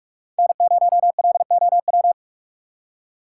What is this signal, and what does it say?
Morse code, heard as a keyed tone, about 46 wpm, 700 Hz. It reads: N0LOW
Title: USA Call Signs - 5 Characters - 3x Character Spacing - 45wpm